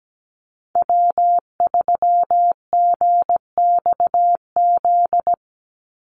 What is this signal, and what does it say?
Morse code, heard as a keyed tone, about 17 wpm, 700 Hz. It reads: W3GXZ
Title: USA Call Signs 17wpm